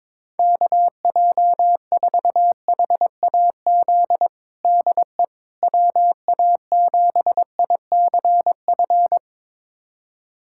Morse code, heard as a keyed tone, about 21 wpm, 700 Hz. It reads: KJ4HAZ DE WA7ICF
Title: USA Callsign DE Callsign - 6 Characters - 22wpm